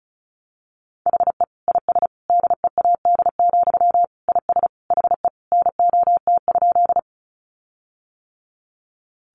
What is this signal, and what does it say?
Morse code, heard as a keyed tone, about 34 wpm, 700 Hz. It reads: HE IS DEAD, IS HE NOT?
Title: Sentences from Top 1500 Words 35wpm